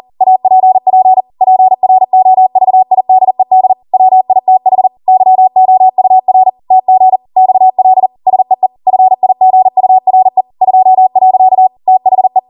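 Morse code, heard as a keyed tone, about 40 wpm, 750 Hz. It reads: APP PROVIDED WITH YOUR TG XL SEE FIGURE 2. THE
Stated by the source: ARRL 40 WPM Morse code practice transmission.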